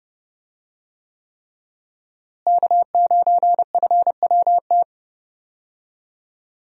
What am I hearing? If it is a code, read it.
Morse code, heard as a keyed tone, about 30 wpm, 700 Hz. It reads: K9FWT
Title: USA Call Signs - 5 Characters - Repeated 3x - 30wpm